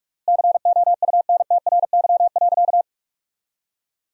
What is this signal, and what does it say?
Morse code, heard as a keyed tone, about 46 wpm, 700 Hz. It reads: COUNTRY.
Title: 3x Word-Spacing - Sentences from Top 300 Words - 45wpm